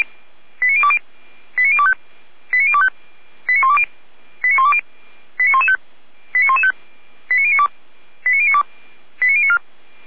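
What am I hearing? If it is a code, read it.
Five-tone selective call (ZVEI), heard as quick runs of five tones, tones of about 70 ms. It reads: 89020 89025 89025 89120 89120 89115 89115 89092 89092 89094
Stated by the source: ZVEI five-tone selective call received on SDR.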